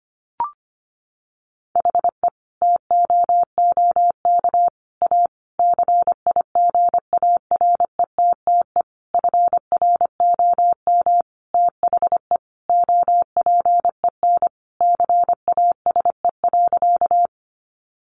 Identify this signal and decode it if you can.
Morse code, heard as a keyed tone, about 25 wpm, 700 Hz. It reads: HE TOOK A CIGARETTE FROM THE OPEN CASE.
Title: Sentences from Top 1000 Words 25wpm